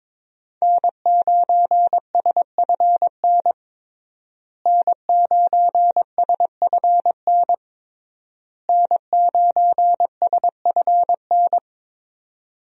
Morse code, heard as a keyed tone, about 22 wpm, 700 Hz. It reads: N9SFN N9SFN N9SFN
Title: USA Call Signs - 5 Characters - Repeated 3x - 22wpm